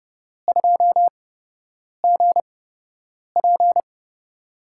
Morse code, heard as a keyed tone, about 30 wpm, spaced as for 8 wpm, 700 Hz. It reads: 2GP
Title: USA Call Signs - 5 Characters - 8x Character Spacing - 30wpm